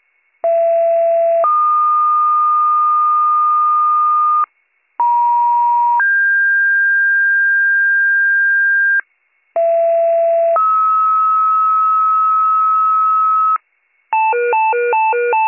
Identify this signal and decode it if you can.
Two-tone page, heard as a one-second tone, then a three-second tone, about 670 then 1220 Hz, about 950 then 1600 Hz, about 670 then 1290 Hz. It is heard from 0.4 to 13.6 s.